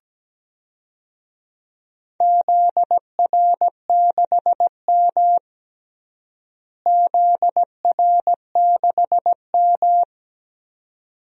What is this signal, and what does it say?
Morse code, heard as a keyed tone, about 17 wpm, 700 Hz. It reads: ZR6M ZR6M
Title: DXCC-100 Call Signs - 4 Characters - Repeated 2x - 17wpm